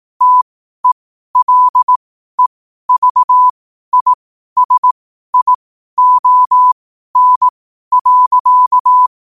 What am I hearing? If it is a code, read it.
Morse code, heard as a keyed tone, about 18 wpm, spaced as for 12 wpm, 1000 Hz. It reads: TELEVISION.
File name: qotdfiles / 12wpm